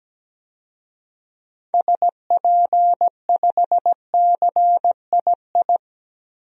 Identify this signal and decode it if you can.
Morse code, heard as a keyed tone, about 17 wpm, 700 Hz. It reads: SP5CII
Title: DXCC-100 Call Signs - 6 Characters - 17wpm